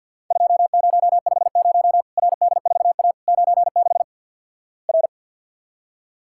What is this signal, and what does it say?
Morse code, heard as a keyed tone, about 53 wpm, 700 Hz. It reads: 2050 RD4A 96 R
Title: Worked All Europe DX Contest - QTC Exchange with Repeats - 50wpm